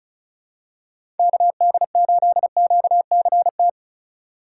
Morse code, heard as a keyed tone, about 36 wpm, 700 Hz. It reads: KD8QCT
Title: USA Call Signs - 4x Character Spacing - 35wpm